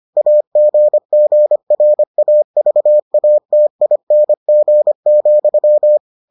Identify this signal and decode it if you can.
Morse code, heard as a keyed tone, about 25 wpm, 600 Hz. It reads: AGGRAVATING,